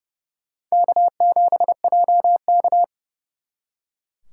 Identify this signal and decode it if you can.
Morse code, heard as a keyed tone, about 36 wpm, 700 Hz. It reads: K7JK
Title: Speed Racing - USA 1x2 Call Signs 30wpm